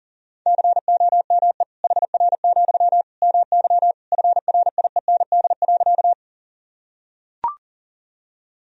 Morse code, heard as a keyed tone, about 40 wpm, 700 Hz. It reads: COME HR, MY FRIEND.
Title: Sentences from Top 300 Words and Abbreviations 40wpm